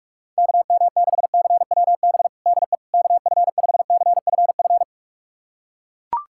Morse code, heard as a keyed tone, about 42 wpm, 700 Hz. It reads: KM6CWB DE KF5CFF
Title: USA Callsign DE Callsign 45wpm